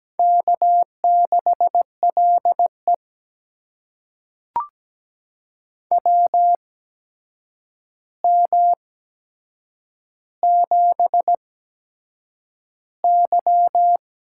Morse code, heard as a keyed tone, about 17 wpm, spaced as for 7 wpm, 700 Hz. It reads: K6LE WM7Y
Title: USA Call Signs - 4 Characters - 8x Character Spacing - 17wpm